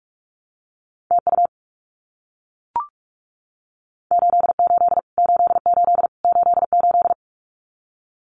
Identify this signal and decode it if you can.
Morse code, heard as a keyed tone, about 44 wpm, 700 Hz. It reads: TU 88 88 88
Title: Abbreviations Repeated 3x - 45wpm